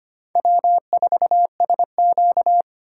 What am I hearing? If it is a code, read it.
Morse code, heard as a keyed tone, about 25 wpm, 700 Hz. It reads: W4SQ
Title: USA Call Signs - 4 Characters - 2x Character Spacing - 25wpm